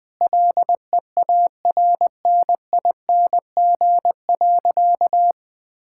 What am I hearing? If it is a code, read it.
Morse code, heard as a keyed tone, about 20 wpm, 700 Hz. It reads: LEARNING.
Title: Sentences with 7-Word Phrases - 20wpm